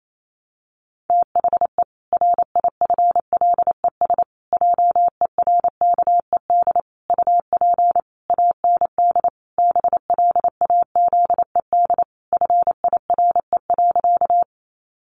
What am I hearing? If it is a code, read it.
Morse code, heard as a keyed tone, about 28 wpm, 700 Hz. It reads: THE RIFLES JERKED UP AND BLAZED FIRE.
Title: A Sound Of Thunder 28wpm